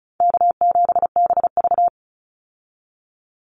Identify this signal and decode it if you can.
Morse code, heard as a keyed tone, about 34 wpm, 700 Hz. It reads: K7BV
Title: Easy Call Signs Repeated 2x - 35wpm